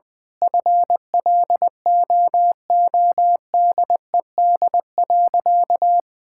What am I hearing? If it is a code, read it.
Morse code, heard as a keyed tone, about 20 wpm, 700 Hz. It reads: FLOODED.
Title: Sentences with 6-Word Phrases - 20wpm